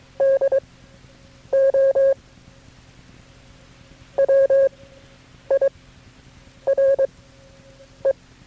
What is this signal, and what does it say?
Morse code, heard as a keyed tone, about 22 wpm, spaced as for 8 wpm, 540 Hz. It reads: DO WIRE
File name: KB6MT-tape3-side1